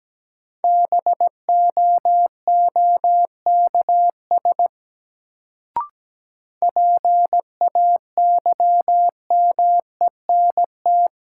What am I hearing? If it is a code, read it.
Morse code, heard as a keyed tone, about 17 wpm, 700 Hz. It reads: BOOKS PAYMENT
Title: Sentences from Top 1200 Words - New Word Drill Alt - 17wpm